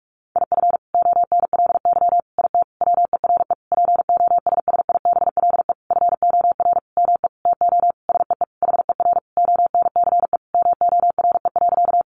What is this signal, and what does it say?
Morse code, heard as a keyed tone, about 44 wpm, 700 Hz. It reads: IF ONLY IT WERE POSSIBLE FOR ME TO SEE HER ONCE MORE.
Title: Sentences from Top 500 Words 45wpm